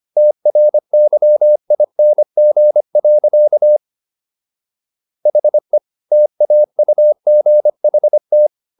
Morse code, heard as a keyed tone, about 25 wpm, 600 Hz. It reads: TRYING. HE TAUGHT